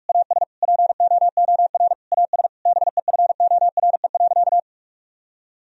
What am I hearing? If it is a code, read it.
Morse code, heard as a keyed tone, about 42 wpm, 700 Hz. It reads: AS POOR AS BEFORE.
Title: Sentences from Top 1000 Words - Review Part 2 - 45wpm